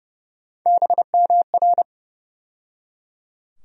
Morse code, heard as a keyed tone, about 30 wpm, 700 Hz. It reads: BMR
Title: USA Call Sign Postfixes - 3 Letters 30wpm